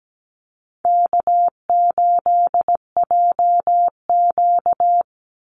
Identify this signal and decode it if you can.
Morse code, heard as a keyed tone, about 17 wpm, 700 Hz. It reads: K8JQ
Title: Easy Call Signs Repeated 3x - 17wpm